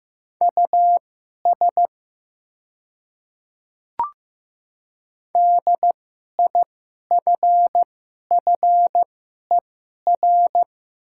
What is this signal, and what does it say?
Morse code, heard as a keyed tone, about 15 wpm, spaced as for 11 wpm, 700 Hz. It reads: US DIFFER
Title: 2x Character Spacing - Top 200 Words - 15wpm